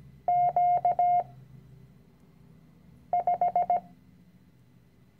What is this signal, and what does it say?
Morse code, heard as a keyed tone, about 17 wpm, 710 Hz. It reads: Q 5
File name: CW-Online-Kurs / Anfänger / Lektion 010